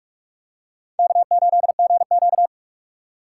Morse code, heard as a keyed tone, about 45 wpm, 700 Hz. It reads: K8GQ
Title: USA Call Signs - 4 Characters - 4x Character Spacing - 45wpm